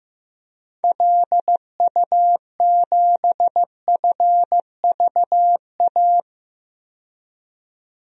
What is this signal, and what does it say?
Morse code, heard as a keyed tone, about 15 wpm, 700 Hz. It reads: LU7FVA
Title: DXCC-100 Call Signs - 6 Characters - 15wpm